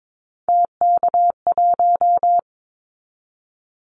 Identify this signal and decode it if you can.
Morse code, heard as a keyed tone, about 22 wpm, 700 Hz. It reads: TK1
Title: Callsign Prefixes for Top 100 DXCC - 22wpm